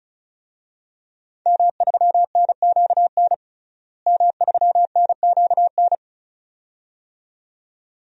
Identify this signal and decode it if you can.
Morse code, heard as a keyed tone, about 36 wpm, 700 Hz. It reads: M3NQN M3NQN
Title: DXCC-100 Call Signs - 5 Characters - Repeated 2x - 35wpm